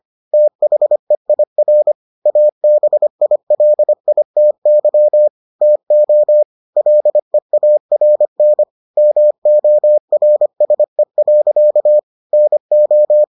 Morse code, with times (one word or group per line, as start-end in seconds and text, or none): 0.3-1.9 THEIR
2.3-5.3 ABILITY
5.6-6.4 TO
6.8-8.6 LEARN
9.0-12.0 MORSE.
12.3-13.2 NO